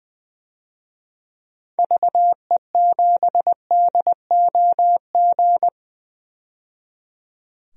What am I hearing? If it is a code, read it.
Morse code, heard as a keyed tone, about 20 wpm, 700 Hz. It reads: VE7DOG
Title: DXCC-100 Call Signs - 6 Characters - 20wpm